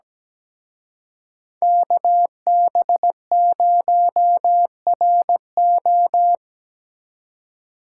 Morse code, heard as a keyed tone, about 19 wpm, 700 Hz. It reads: KB0RO
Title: Speed Racing - USA 2x2 Call Signs 17wpm